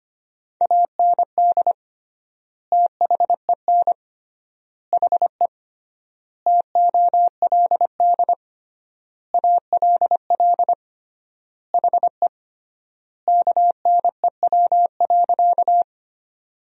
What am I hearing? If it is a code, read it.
Morse code, heard as a keyed tone, about 25 wpm, 700 Hz. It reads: AND THEN HE TOLD ALL HE KNEW.
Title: 3x Word-Spacing - Sentences from Top 400 Words - 25wpm